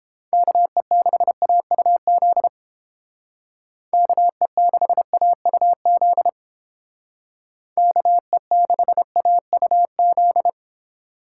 Morse code, heard as a keyed tone, about 28 wpm, 700 Hz. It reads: KE6AUZ KE6AUZ KE6AUZ